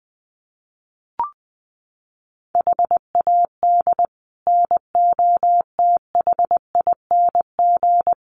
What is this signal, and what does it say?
Morse code, heard as a keyed tone, about 20 wpm, 700 Hz. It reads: HAD NOTHING
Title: Sets of 5 Words 20wpm